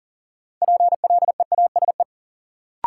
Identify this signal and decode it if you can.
Morse code, heard as a keyed tone, about 40 wpm, 700 Hz. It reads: PLEASE
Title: Sentences from Top 1500 Words - New Word Drill - 40wpm